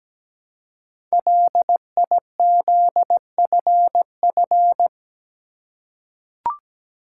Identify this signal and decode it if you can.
Morse code, heard as a keyed tone, about 17 wpm, 700 Hz. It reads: LIZFF
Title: Code Groups - 5 Letters - 17wpm